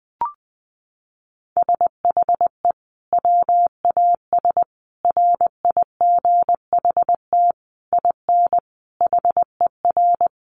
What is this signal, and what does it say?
Morse code, heard as a keyed tone, about 20 wpm, 700 Hz. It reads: SHE WAS RIGHT IN HER
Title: Sentences from Top 1500 Words 20wpm